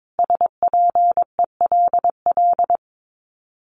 Morse code, heard as a keyed tone, about 22 wpm, 700 Hz. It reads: SPELL
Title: Top 500 Words Repeated 22wpm